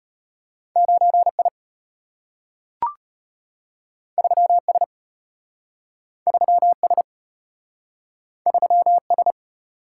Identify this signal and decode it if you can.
Morse code, heard as a keyed tone, about 34 wpm, 700 Hz. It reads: 9I 3S 3S 3S